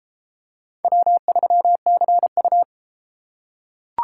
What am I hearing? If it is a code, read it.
Morse code, heard as a keyed tone, about 32 wpm, 700 Hz. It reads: W3CU